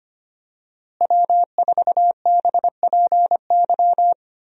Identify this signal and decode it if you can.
Morse code, heard as a keyed tone, about 25 wpm, 700 Hz. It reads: W4BPY
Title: USA Call Signs - 5 Characters - Repeated 3x - 25wpm